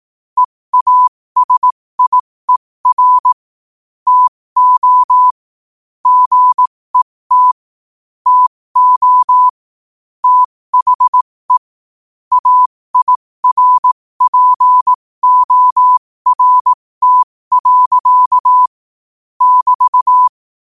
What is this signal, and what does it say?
Morse code, heard as a keyed tone, about 18 wpm, spaced as for 15 wpm, 1000 Hz. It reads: EASIER TO GET TO THE AIRPORT. <BT>
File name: qotdfiles / 15wpm